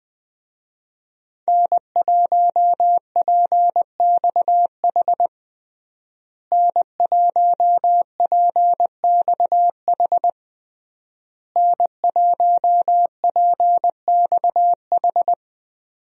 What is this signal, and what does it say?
Morse code, heard as a keyed tone, about 20 wpm, 700 Hz. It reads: N1PXH N1PXH N1PXH
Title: USA Call Signs - 5 Characters - Repeated 3x - 20wpm